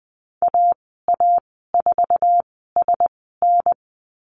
Morse code, heard as a keyed tone, about 20 wpm, spaced as for 15 wpm, 700 Hz. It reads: AA4SN
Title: 062 Lesson 11g - Letter h - Call Signs 20wpm